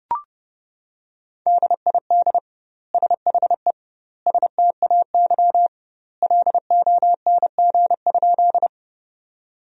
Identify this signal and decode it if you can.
Morse code, heard as a keyed tone, about 30 wpm, 700 Hz. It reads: DID SHE STAY LONG?
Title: 2x Word-Spacing - Sentences from Top 500 Words - 30wpm